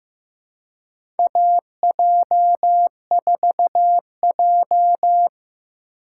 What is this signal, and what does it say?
Morse code, heard as a keyed tone, about 15 wpm, 700 Hz. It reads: AJ4J
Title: USA Call Signs - 4 Characters - 2x Character Spacing - 15wpm